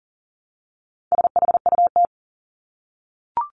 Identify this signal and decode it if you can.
Morse code, heard as a keyed tone, about 39 wpm, 700 Hz. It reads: SHUT
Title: Top 1751-2000 Words 40wpm